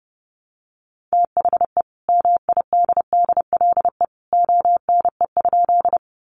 Morse code, heard as a keyed tone, about 30 wpm, 700 Hz. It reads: THE MIDDLE ONE?
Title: Sentences from Top 900 Words 30wpm